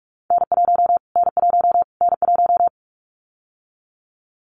Morse code, heard as a keyed tone, about 44 wpm, 700 Hz. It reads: N1 N1 N1
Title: Callsign Prefixes for Top 25 DXCC - Repeated 3x - 45wpm